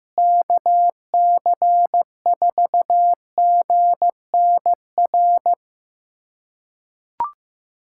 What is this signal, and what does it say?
Morse code, heard as a keyed tone, about 15 wpm, 700 Hz. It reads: KC4GNR
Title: USA Callsign DE Callsign - 6 Characters - 15wpm